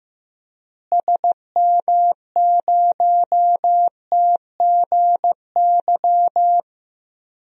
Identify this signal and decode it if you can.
Morse code, heard as a keyed tone, about 16 wpm, 700 Hz. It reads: SM0TGY
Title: DXCC-100 Call Signs - 6 Characters - Repeated 2x - 15wpm